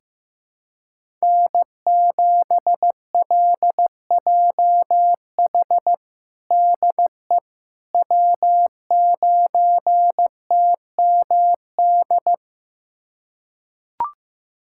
Morse code, heard as a keyed tone, about 15 wpm, 700 Hz. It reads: N7LJH DE W9TMD
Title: USA Callsign DE Callsign - 5 Characters - 15wpm